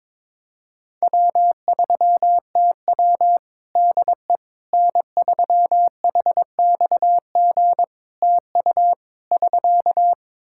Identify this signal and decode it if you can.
Morse code, heard as a keyed tone, about 22 wpm, 700 Hz. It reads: W3TW DE N3HXG TU <SK>
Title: Practice QSO - Exchange 4 - Ending - 22wpm